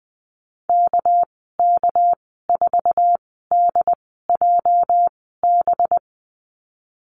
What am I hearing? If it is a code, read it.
Morse code, heard as a keyed tone, about 20 wpm, spaced as for 14 wpm, 700 Hz. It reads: KK4DJB
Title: 213 Lesson 34g - Number 8 - Call Signs 20wpm